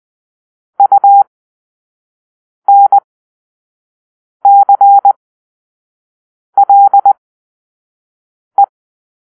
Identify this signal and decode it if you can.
Morse code, heard as a keyed tone, about 20 wpm, spaced as for 5 wpm, 800 Hz. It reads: UNCLE
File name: Grp-3 Drill-1